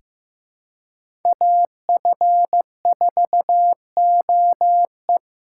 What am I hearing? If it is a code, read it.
Morse code, heard as a keyed tone, about 15 wpm, 700 Hz. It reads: AF4OE
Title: USA Call Signs - 4x Character Spacing - 15wpm